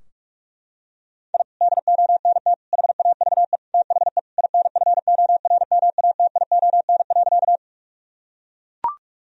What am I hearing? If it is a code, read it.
Morse code, heard as a keyed tone, about 41 wpm, 700 Hz. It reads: I DONT HAVE THE INFORMATION.
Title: Sentences from Top 1000 Words - Review Part 3 - 45wpm